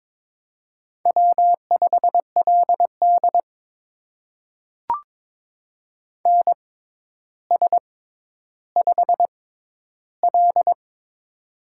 Morse code, heard as a keyed tone, about 22 wpm, spaced as for 11 wpm, 700 Hz. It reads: W5LD NS5L
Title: USA Call Signs - 4 Characters - 6x Character Spacing - 22wpm